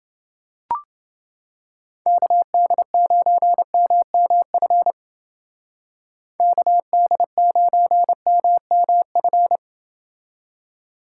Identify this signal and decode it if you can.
Morse code, heard as a keyed tone, about 30 wpm, 700 Hz. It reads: KD9MMF KD9MMF